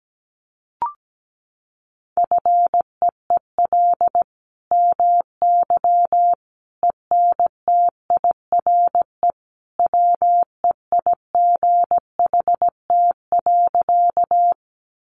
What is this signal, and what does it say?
Morse code, heard as a keyed tone, about 17 wpm, 700 Hz. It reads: FEEL MY ENTIRE WEIGHT.
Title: Sentences from Top 800 Words 17wpm